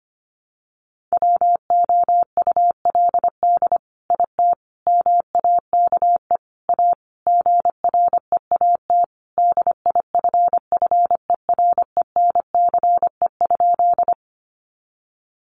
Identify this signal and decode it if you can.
Morse code, heard as a keyed tone, about 25 wpm, 700 Hz. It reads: WOULD IT MAKE A GREAT DIFFERENCE?
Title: Sentences from Top 800 Words 25wpm